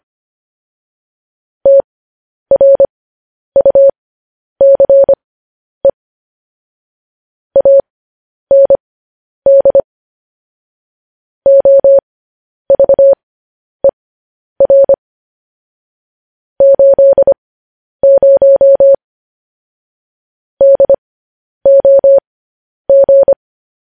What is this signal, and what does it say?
Morse code, heard as a keyed tone, about 25 wpm, spaced as for 10 wpm, 550 Hz. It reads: TRUCE AND OVER 80 DOG
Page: news headlines in morse code at 10 wpm